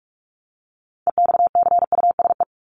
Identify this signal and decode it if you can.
Morse code, heard as a keyed tone, about 44 wpm, 700 Hz. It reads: EXCUSE
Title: Top 501-750 Words 45wpm